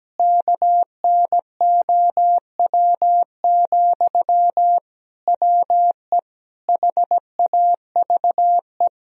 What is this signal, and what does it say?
Morse code, heard as a keyed tone, about 17 wpm, 700 Hz. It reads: KNOW, WE HAVE
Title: Sentences from Top 1000 Words - Review Part 3 - 17wpm